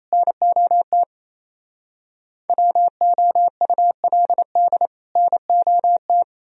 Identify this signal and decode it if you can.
Morse code, heard as a keyed tone, about 34 wpm, 700 Hz. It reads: NOT WOULD NOT
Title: Speed-Racing - Sets of 2 Words 28wpm